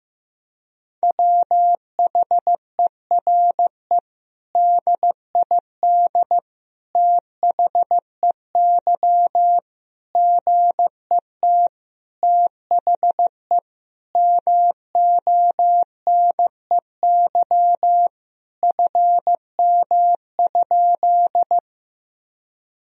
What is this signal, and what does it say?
Morse code, heard as a keyed tone, about 15 wpm, 700 Hz. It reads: WHERE DID THEY GET THE MONEY FM?
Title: Sentences from Top 400 Words and Abbreviations 15wpm